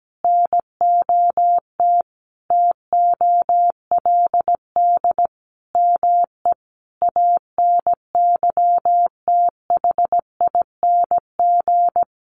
Morse code, heard as a keyed tone, about 17 wpm, 700 Hz. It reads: NOT TOLD ME ANYTHING
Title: Sentences from Top 1000 Words 17wpm